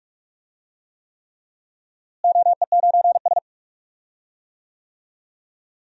Morse code, heard as a keyed tone, about 46 wpm, 700 Hz. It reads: OE9S
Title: DXCC-100 Call Signs - 4 Characters - Repeated 3x - 45wpm